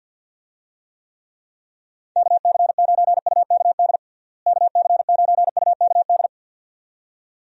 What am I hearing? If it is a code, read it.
Morse code, heard as a keyed tone, about 51 wpm, 700 Hz. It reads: KC9UKD KC9UKD
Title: USA Call Signs - 6 Characters - Repeated 2x - 50wpm